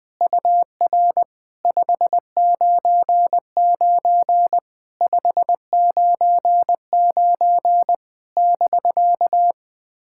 Morse code, heard as a keyed tone, about 21 wpm, 700 Hz. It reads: UR 599 599 <BK>
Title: POTA QSO Practice - 20wpm